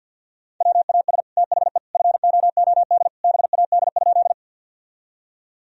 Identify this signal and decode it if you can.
Morse code, heard as a keyed tone, about 52 wpm, 700 Hz. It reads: WAS THE FOOD BAD?
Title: Sentences from Top 1000 Words - Review Part 1 - 50wpm